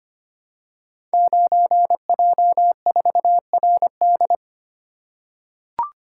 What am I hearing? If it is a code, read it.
Morse code, heard as a keyed tone, about 25 wpm, 700 Hz. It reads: 9J4RD
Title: Code Groups - 5 Letter:Numbers - 25wpm